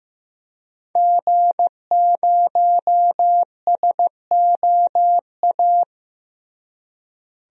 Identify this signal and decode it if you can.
Morse code, heard as a keyed tone, about 15 wpm, 700 Hz. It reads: G0SOA
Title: DXCC-100 Call Signs - 5 Characters - Repeated 3x - 15wpm